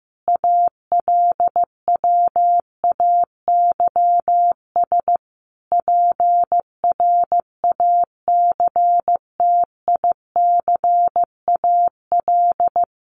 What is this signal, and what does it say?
Morse code, heard as a keyed tone, about 15 wpm, 700 Hz. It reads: ALWAYS PRACTICAL
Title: Sentences from Top 1400 Words 15wpm